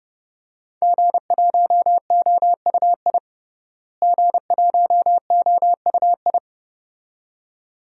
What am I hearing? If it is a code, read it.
Morse code, heard as a keyed tone, about 31 wpm, 700 Hz. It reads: G1OUI G1OUI
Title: DXCC-100 Call Signs - 5 Characters - Repeated 2x - 30wpm